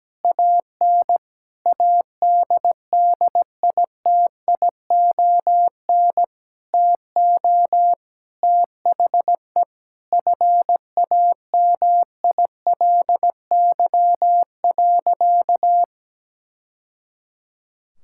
Morse code, heard as a keyed tone, about 17 wpm, 700 Hz. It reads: AN ADDITION TO THE FAMILY.
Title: Sentences from Top 1000 Words - Review Part 6 - 17wpm